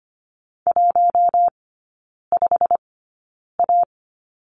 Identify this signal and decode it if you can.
Morse code, heard as a keyed tone, about 25 wpm, spaced as for 9 wpm, 700 Hz. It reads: 15A
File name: Lesson 17e - Letter u - Three Characters 25wpm